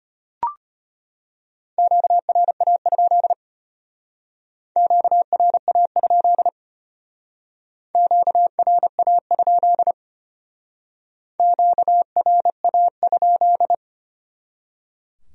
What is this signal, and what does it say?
Morse code, heard as a keyed tone, about 32 wpm, 700 Hz. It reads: QRA? QRA? QRA? QRA?